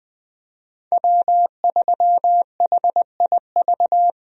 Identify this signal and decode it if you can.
Morse code, heard as a keyed tone, about 20 wpm, 700 Hz. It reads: W3HIV
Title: USA Call Signs - 5 Characters - 20wpm